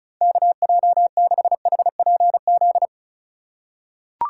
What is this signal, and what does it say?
Morse code, heard as a keyed tone, about 36 wpm, 700 Hz. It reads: KJ6HPZ
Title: USA Call Signs - 6 Characters - 2x Character Spacing - 35wpm